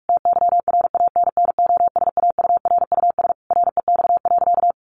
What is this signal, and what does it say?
Morse code, heard as a keyed tone, about 44 wpm, 700 Hz. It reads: TYRANNOSAURUS REX.
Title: A Sound Of Thunder 45wpm